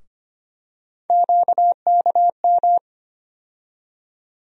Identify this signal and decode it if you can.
Morse code, heard as a keyed tone, about 25 wpm, 700 Hz. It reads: QKM
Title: USA Call Sign Postfixes - 3 Letters 25wpm